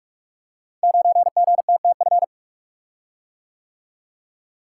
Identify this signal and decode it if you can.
Morse code, heard as a keyed tone, about 44 wpm, 700 Hz. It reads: 9GTTF
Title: Code Groups - 5 Letter:Numbers - 45wpm